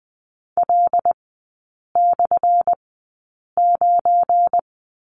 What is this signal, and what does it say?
Morse code, heard as a keyed tone, about 20 wpm, spaced as for 8 wpm, 700 Hz. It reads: L/9
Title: 191 Lesson 31e - Letter v - Three Characters 20wpm